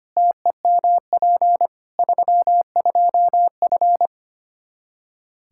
Morse code, heard as a keyed tone, about 25 wpm, 700 Hz. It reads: TEMP 32F
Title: Practice QSO - Exchange 2 - Info - 25wpm